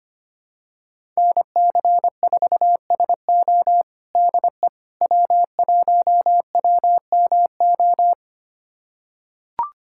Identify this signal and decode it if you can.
Morse code, heard as a keyed tone, about 25 wpm, 700 Hz. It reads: NC4SO DE W1WMO
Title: USA Callsign DE Callsign - 5 Characters - 25wpm